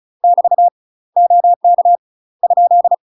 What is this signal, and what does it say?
Morse code, heard as a keyed tone, about 34 wpm, 700 Hz. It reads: <BT> OK ?
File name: UK MASTER QSOs TRAINING / 35WPM / UK MASTER QSOs 0001